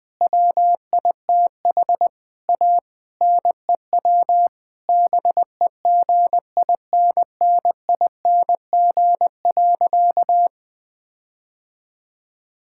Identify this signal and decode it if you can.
Morse code, heard as a keyed tone, about 20 wpm, 700 Hz. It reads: WITH A NEW BEGINNING.
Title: Difficult Sentences with Hints - 20wpm